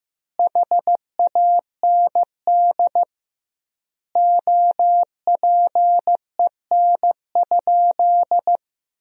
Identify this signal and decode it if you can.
Morse code, heard as a keyed tone, about 15 wpm, 700 Hz. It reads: HAND OPEN?
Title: 2x Word-Spacing - Sentences from Top 300 Words - 15wpm